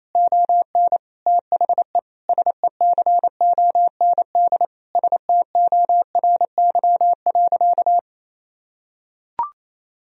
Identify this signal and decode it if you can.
Morse code, heard as a keyed tone, about 28 wpm, 700 Hz. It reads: ON THE SECOND STORY.
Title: Sentences from Top 1000 Words - Review Part 3 - 28wpm